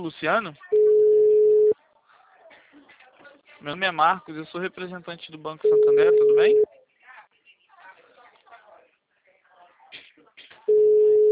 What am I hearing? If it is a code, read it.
Ringback tone (Europe, 425 Hz), with speech.